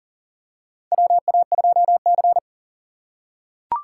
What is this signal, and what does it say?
Morse code, heard as a keyed tone, about 40 wpm, 700 Hz. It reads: WA2C
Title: USA Call Signs - 4 Characters - 2x Character Spacing - 40wpm